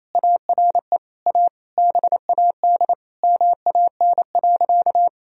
Morse code, heard as a keyed tone, about 28 wpm, 700 Hz. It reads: ARE A BAD MAN.
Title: Sentences from Top 1000 Words - Review Part 1 - 28wpm